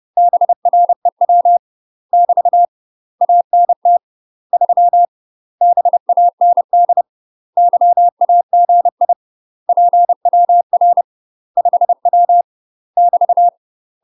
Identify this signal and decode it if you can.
Morse code, heard as a keyed tone, about 30 wpm, 700 Hz. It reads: BREW <BT> ANT 3 BAND YAGI PWR 5W <BT>
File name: UK MASTER QSOs TRAINING / 30WPM / UK MASTER QSOs 0012